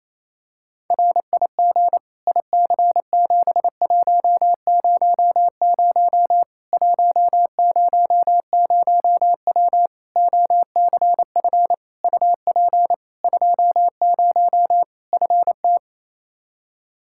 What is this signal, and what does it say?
Morse code, heard as a keyed tone, about 28 wpm, 700 Hz. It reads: RIG IC7100 100W OCF UP 20 FT
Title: Practice QSO - Exchange 2 - Info - 28wpm